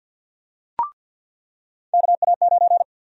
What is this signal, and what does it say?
Morse code, heard as a keyed tone, about 53 wpm, 700 Hz. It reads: KA9
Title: USA Call Sign Prefixes - 50wpm